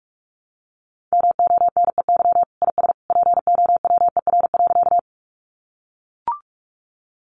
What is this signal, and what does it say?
Morse code, heard as a keyed tone, about 44 wpm, 700 Hz. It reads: MONEY IS POWER.
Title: Sentences from Top 500 Words 45wpm